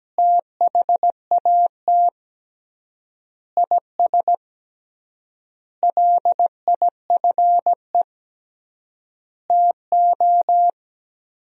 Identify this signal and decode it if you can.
Morse code, heard as a keyed tone, about 17 wpm, 700 Hz. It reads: THAT IS LIFE TO
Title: 3x Word-Spacing - Sentences from Top 500 Words - 17wpm